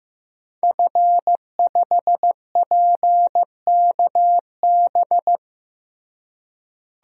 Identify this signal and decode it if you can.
Morse code, heard as a keyed tone, about 15 wpm, 700 Hz. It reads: F5PKB
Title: DXCC-100 Call Signs - 5 Characters - Repeated 3x - 15wpm